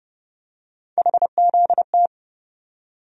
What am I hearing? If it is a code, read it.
Morse code, heard as a keyed tone, about 30 wpm, 700 Hz. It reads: HZT
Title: USA Call Sign Postfixes - 3 Letters 30wpm